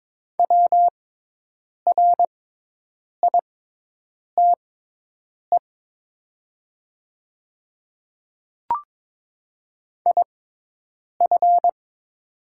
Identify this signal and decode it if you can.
Morse code, heard as a keyed tone, about 22 wpm, spaced as for 8 wpm, 700 Hz. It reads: WRITE IF
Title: 6x Character Spacing - Top 100 Words - 22wpm